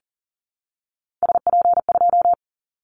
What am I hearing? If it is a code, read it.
Morse code, heard as a keyed tone, about 40 wpm, 700 Hz. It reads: SP2
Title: Callsign Prefixes for Top 25 DXCC - 40wpm